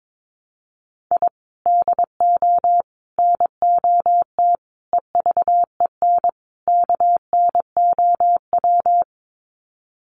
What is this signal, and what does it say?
Morse code, heard as a keyed tone, about 22 wpm, 700 Hz. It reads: I DO NOT EVEN KNOW
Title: Sets of 5 Words 22wpm